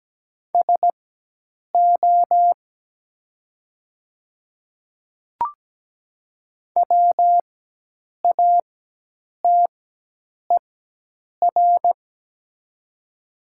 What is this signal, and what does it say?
Morse code, heard as a keyed tone, about 17 wpm, spaced as for 8 wpm, 700 Hz. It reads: SO WATER
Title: 4x Character Spacing - Top 100 Words - 17wpm